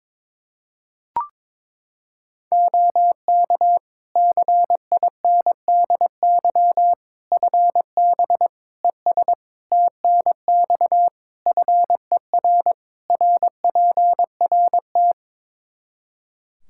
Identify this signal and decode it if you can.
Morse code, heard as a keyed tone, about 22 wpm, 700 Hz. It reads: OK CINDY FB ES TNX FER RPRT
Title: Practice QSO - Exchange 2 - Info - 22wpm